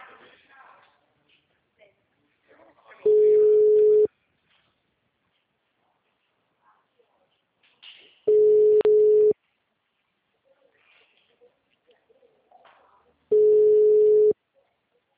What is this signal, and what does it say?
Ringback tone (Europe, 425 Hz), with speech.